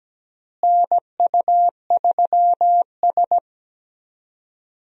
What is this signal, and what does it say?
Morse code, heard as a keyed tone, about 17 wpm, 700 Hz. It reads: NU3S
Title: USA Call Signs - 4 Characters - Repeated 3x - 17wpm